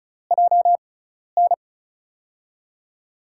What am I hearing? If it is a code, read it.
Morse code, heard as a keyed tone, about 36 wpm, spaced as for 12 wpm, 700 Hz. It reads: JN
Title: USA Call Signs - 6 Characters - 6x Character Spacing - 35wpm